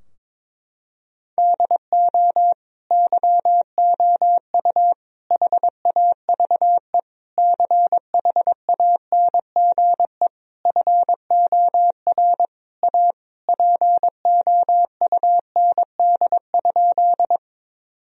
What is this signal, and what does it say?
Morse code, heard as a keyed tone, about 21 wpm, 700 Hz. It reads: DO YOU HAVE CHANGE FOR A POUND?
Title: Sentences from Top 1000 Words - Review Part 2 - 22wpm